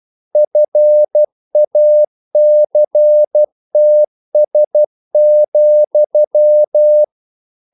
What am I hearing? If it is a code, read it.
Morse code, heard as a keyed tone, about 12 wpm, 600 Hz. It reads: FACTS,
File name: CW / Practice / EnglishQuotations0112